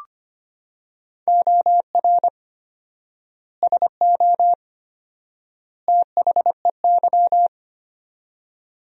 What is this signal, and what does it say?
Morse code, heard as a keyed tone, about 25 wpm, 700 Hz. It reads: OR SO THEY
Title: 4x Word-Spacing - Sentences from Top 100 Words - 25wpm